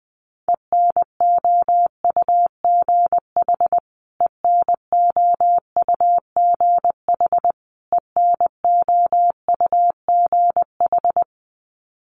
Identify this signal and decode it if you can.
Morse code, heard as a keyed tone, about 20 wpm, 700 Hz. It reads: ENOUGH ENOUGH ENOUGH
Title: Top 500 Words Repeated 20wpm